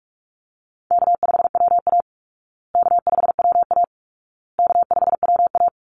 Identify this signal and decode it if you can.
Morse code, heard as a keyed tone, about 44 wpm, 700 Hz. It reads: K5WA K5WA K5WA
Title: Call Signs Repeated 2x - 45wpm